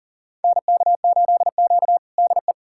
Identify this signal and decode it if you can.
Morse code, heard as a keyed tone, about 40 wpm, 700 Hz. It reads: NK8Q DE
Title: USA Callsign DE Callsign 40wpm